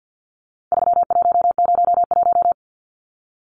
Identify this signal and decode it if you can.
Morse code, heard as a keyed tone, about 50 wpm, 700 Hz. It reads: 3101
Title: Numbers - 4-Digit 50wpm